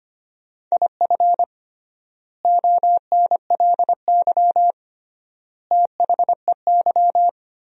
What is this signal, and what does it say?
Morse code, heard as a keyed tone, about 25 wpm, 700 Hz. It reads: IF ONLY THEY
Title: 3x Word-Spacing - Sentences from Top 500 Words - 25wpm